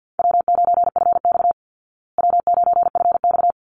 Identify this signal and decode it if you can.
Morse code, heard as a keyed tone, about 50 wpm, 700 Hz. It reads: W9FX W9FX
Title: Easy Call Signs Repeated 2x - 50wpm